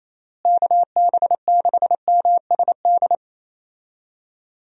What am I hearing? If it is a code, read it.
Morse code, heard as a keyed tone, about 28 wpm, 700 Hz. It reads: KB6MSD
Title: USA Call Signs - 6 Characters - Repeated 3x - 28wpm